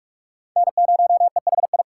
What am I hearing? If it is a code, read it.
Morse code, heard as a keyed tone, about 44 wpm, 700 Hz. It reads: N0EHI